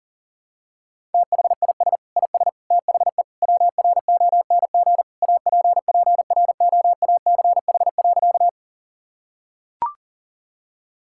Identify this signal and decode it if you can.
Morse code, heard as a keyed tone, about 40 wpm, 700 Hz. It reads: THIS IS THE WRONG APPROACH.
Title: Sentences from Top 1000 Words - Review Part 6 - 40wpm